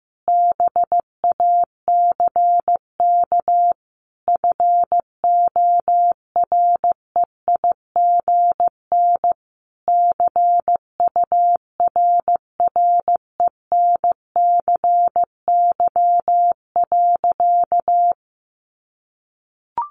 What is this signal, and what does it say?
Morse code, heard as a keyed tone, about 15 wpm, 700 Hz. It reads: BACK FOREIGN CURRENCY.
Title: Sentences from Top 1500 Words 15wpm